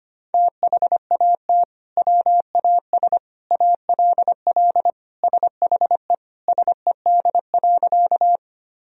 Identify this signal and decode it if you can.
Morse code, heard as a keyed tone, about 25 wpm, 700 Hz. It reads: THAT WAS ALL SHE SED.
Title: Sentences from Top 100 Words and Abbreviations 25wpm